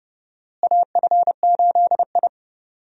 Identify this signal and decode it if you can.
Morse code, heard as a keyed tone, about 30 wpm, 700 Hz. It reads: AF8I
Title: USA Call Signs - 4 Characters - Repeated 3x - 30wpm